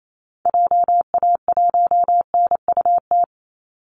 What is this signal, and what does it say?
Morse code, heard as a keyed tone, about 28 wpm, 700 Hz. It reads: JA1NUT
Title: Call Signs Repeated 2x - 28wpm